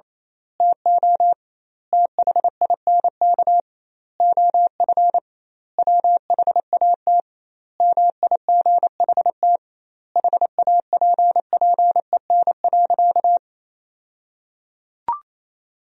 Morse code, heard as a keyed tone, about 28 wpm, 700 Hz. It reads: TO THINK OF WHAT MIGHT HAPPEN.
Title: 2x Word-Spacing - Sentences from Top 400 Words - 28wpm